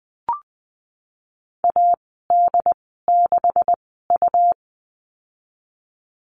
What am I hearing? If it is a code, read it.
Morse code, heard as a keyed tone, about 20 wpm, spaced as for 15 wpm, 700 Hz. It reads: AD6U
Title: 140 Lesson 23g - Letter f - Call Signs 20wpm